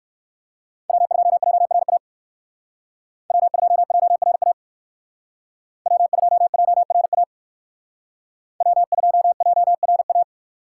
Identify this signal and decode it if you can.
Morse code, heard as a keyed tone, about 58 wpm, 700 Hz. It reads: W2JRA W2JRA W2JRA W2JRA